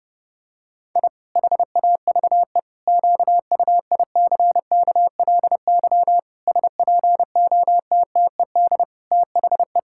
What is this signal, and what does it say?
Morse code, heard as a keyed tone, about 30 wpm, 700 Hz. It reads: I HAVE QUICKLY SPOTTED THE
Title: Pangram Challenge v2 - 30wpm